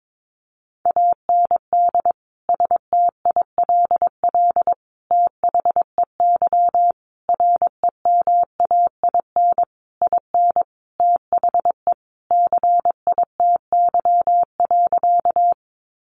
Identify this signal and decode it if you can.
Morse code, heard as a keyed tone, about 22 wpm, 700 Hz. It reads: AND STILL THEY REMAIN IN THE CITY.
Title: Sentences from Top 1200 Words 22wpm